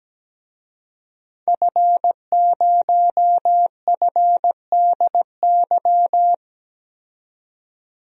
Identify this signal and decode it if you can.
Morse code, heard as a keyed tone, about 17 wpm, 700 Hz. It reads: F0FDY
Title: DXCC-100 Call Signs - 5 Characters - 17wpm